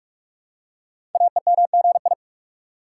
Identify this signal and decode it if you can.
Morse code, heard as a keyed tone, about 44 wpm, 700 Hz. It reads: AEMGI
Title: Code Groups - 5 Letters - 45wpm